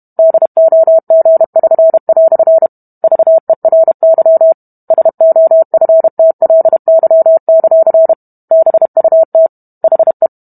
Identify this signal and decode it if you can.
Morse code, heard as a keyed tone, about 32 wpm, 650 Hz. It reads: DOG<SN>" VERY SOFTLY; BUT HE